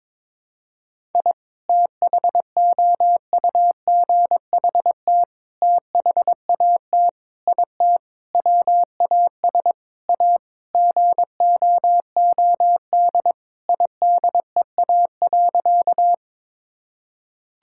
Morse code, heard as a keyed tone, about 22 wpm, 700 Hz. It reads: I THOUGHT THAT IT WAS A GOOD IDEA.
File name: practice / 22wpm / Sentences with 4-Word Phrases - 22wpm